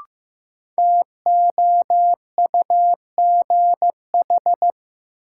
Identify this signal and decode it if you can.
Morse code, heard as a keyed tone, about 16 wpm, 700 Hz. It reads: TOUGH
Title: Sentences from Top 1200 Words - New Word Drill Alt - 15wpm